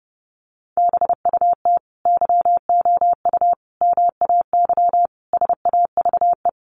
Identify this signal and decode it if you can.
Morse code, heard as a keyed tone, about 28 wpm, 700 Hz. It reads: BUT YOU MAY SAVE
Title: Sentences from Top 1100 Words 30wpm